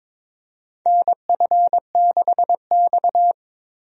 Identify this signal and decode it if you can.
Morse code, heard as a keyed tone, about 22 wpm, 700 Hz. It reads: NF6X
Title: USA Call Signs - 4 Characters - 22wpm